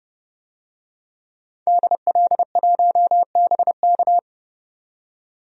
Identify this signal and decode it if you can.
Morse code, heard as a keyed tone, about 30 wpm, 700 Hz. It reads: DL1BK
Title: DXCC-100 Call Signs - 5 Characters - 30wpm